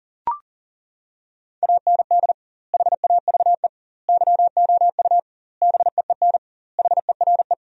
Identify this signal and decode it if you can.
Morse code, heard as a keyed tone, about 40 wpm, 700 Hz. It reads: AND HAVE YOU BEEN HERE
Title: 2x Word-Spacing - Sentences from Top 200 Words - 40wpm